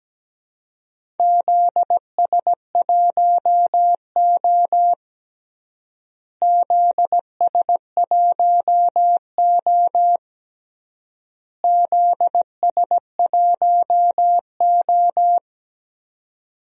Morse code, heard as a keyed tone, about 17 wpm, 700 Hz. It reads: ZS1O ZS1O ZS1O
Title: DXCC-100 Call Signs - 4 Characters - Repeated 3x - 17wpm